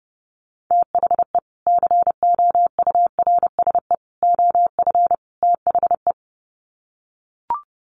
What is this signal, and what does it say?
Morse code, heard as a keyed tone, about 30 wpm, 700 Hz. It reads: THE COURSE OF THE
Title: Sets of 4 Words 30wpm